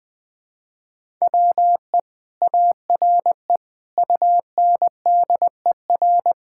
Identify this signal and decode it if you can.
Morse code, heard as a keyed tone, about 21 wpm, 700 Hz. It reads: WE ARE UNDER
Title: Sentences from Top 1000 Words - Review Part 6 - 20wpm